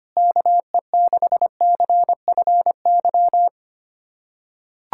Morse code, heard as a keyed tone, about 25 wpm, 700 Hz. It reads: KE6CFY
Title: USA Callsign DE Callsign 25wpm